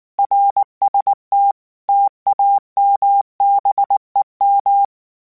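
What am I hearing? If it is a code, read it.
Morse code, heard as a keyed tone, about 19 wpm, 800 Hz. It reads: RST TAMBEM
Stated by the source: Morse code practice text at 18 wpm, from Portal do CW.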